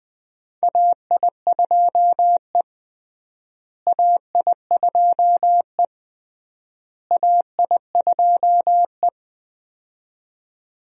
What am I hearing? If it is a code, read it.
Morse code, heard as a keyed tone, about 19 wpm, 700 Hz. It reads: AI2E AI2E AI2E
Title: USA Call Signs - 4 Characters - Repeated 3x - 20wpm